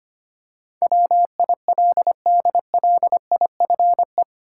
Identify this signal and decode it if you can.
Morse code, heard as a keyed tone, about 25 wpm, 700 Hz. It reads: WILDLIFE
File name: 8 Letter Words - First 3 Letters Given - 25wpm